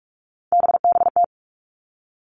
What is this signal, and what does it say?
Morse code, heard as a keyed tone, about 46 wpm, 700 Hz. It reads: BBT
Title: Three Challenging Letter-Number Combinations 45wpm